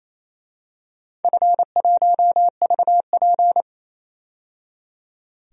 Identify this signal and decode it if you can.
Morse code, heard as a keyed tone, about 28 wpm, 700 Hz. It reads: F1VP
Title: DXCC-100 Call Signs - 4 Characters - 28wpm